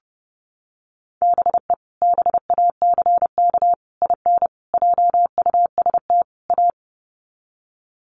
Morse code, heard as a keyed tone, about 30 wpm, 700 Hz. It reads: BE BACK IN JUST A
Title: Sets of 5 Words Encore 30wpm